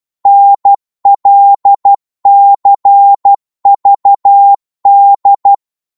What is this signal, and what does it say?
Morse code, heard as a keyed tone, about 12 wpm, 800 Hz. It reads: NLCVD
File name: Morse / ltr-12wpm